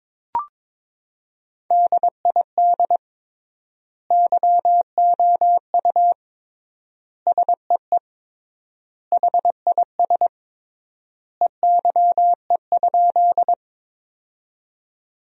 Morse code, heard as a keyed tone, about 22 wpm, 700 Hz. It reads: DID YOU SEE HIS EYE?
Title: 3x Word-Spacing - Sentences from Top 300 Words - 22wpm